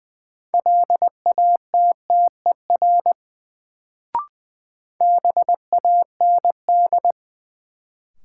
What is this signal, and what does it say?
Morse code, heard as a keyed tone, about 20 wpm, 700 Hz. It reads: LATTER BAND
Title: Sentences from Top 1600 Words - New Word Drill - 20wpm